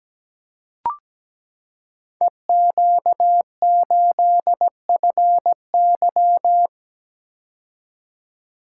Morse code, heard as a keyed tone, about 17 wpm, 700 Hz. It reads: EQ8FY
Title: Code Groups - 5 Letter:Numbers - 17wpm